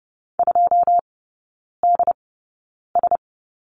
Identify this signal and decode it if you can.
Morse code, heard as a keyed tone, about 30 wpm, spaced as for 9 wpm, 700 Hz. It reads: 2DS
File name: 081 Lesson 14e - Number 2 - Three Characters 30wpm